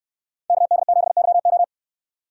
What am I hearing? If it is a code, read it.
Morse code, heard as a keyed tone, about 69 wpm, 700 Hz. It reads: KD7YO